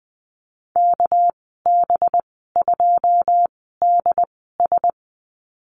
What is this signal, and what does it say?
Morse code, heard as a keyed tone, about 20 wpm, spaced as for 15 wpm, 700 Hz. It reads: KB2DS
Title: 207 Lesson 33g - Letter j - Call Signs 20wpm